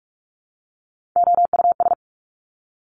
Morse code, heard as a keyed tone, about 46 wpm, 700 Hz. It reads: OUS
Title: Top Three Letter Combinations in Words 45wpm